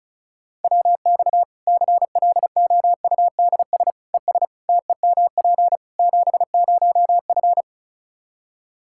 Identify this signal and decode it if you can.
Morse code, heard as a keyed tone, about 36 wpm, 700 Hz. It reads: WX CLOUDS ES TEMP 70F
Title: Practice QSO - Exchange 2 - Info - 35wpm